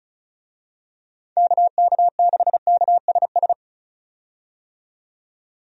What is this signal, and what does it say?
Morse code, heard as a keyed tone, about 36 wpm, 700 Hz. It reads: KK6KSS
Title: USA Call Signs - 6 Characters - 35wpm